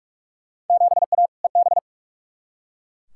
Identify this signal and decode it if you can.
Morse code, heard as a keyed tone, about 42 wpm, 700 Hz. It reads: 7A EB
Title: Field Day Exchanges Encore - 45wpm